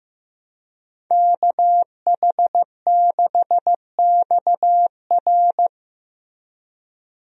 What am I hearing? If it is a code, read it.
Morse code, heard as a keyed tone, about 15 wpm, 700 Hz. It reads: KH6XR
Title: DXCC-100 Call Signs - 5 Characters - Repeated 3x - 15wpm